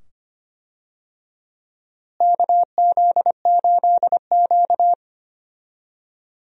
Morse code, heard as a keyed tone, about 25 wpm, 700 Hz. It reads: KZ8Q
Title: USA Call Signs - 4 Characters - Repeated 2x - 25wpm